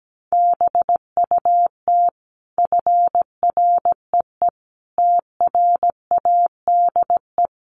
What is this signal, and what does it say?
Morse code, heard as a keyed tone, about 17 wpm, 700 Hz. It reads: BUT FREE TRADE
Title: Sentences from Top 1300 Words 17wpm